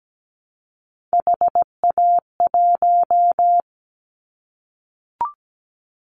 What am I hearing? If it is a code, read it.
Morse code, heard as a keyed tone, about 17 wpm, 700 Hz. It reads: HA1
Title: Callsign Prefixes for Top 25 DXCC - 17wpm